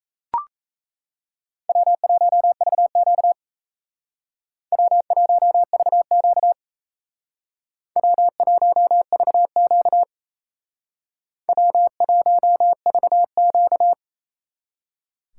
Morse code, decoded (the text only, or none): W1VQ W1VQ W1VQ W1VQ